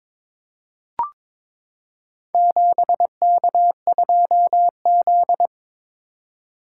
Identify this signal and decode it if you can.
Morse code, heard as a keyed tone, about 22 wpm, 700 Hz. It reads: 7K2Z
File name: practice / 22wpm / Four Letter-Number Combinations - 22wpm